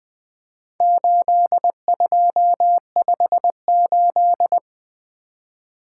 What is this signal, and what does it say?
Morse code, heard as a keyed tone, about 20 wpm, 700 Hz. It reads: 8258
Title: Four Letter-Number Combinations - 20wpm